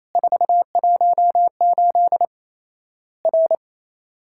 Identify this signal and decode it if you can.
Morse code, heard as a keyed tone, about 28 wpm, 700 Hz. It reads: 418 R
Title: Worked All Europe DX Contest - QTC Exchange - 28wpm